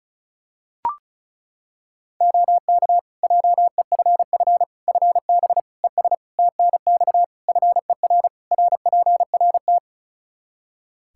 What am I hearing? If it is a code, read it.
Morse code, heard as a keyed tone, about 34 wpm, 700 Hz. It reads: OK JEFF FB ES TNX FER RPRT
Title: K1USN SST QSO Practice 35wpm